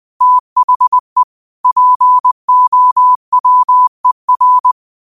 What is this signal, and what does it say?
Morse code, heard as a keyed tone, about 20 wpm, 1000 Hz. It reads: THE POWER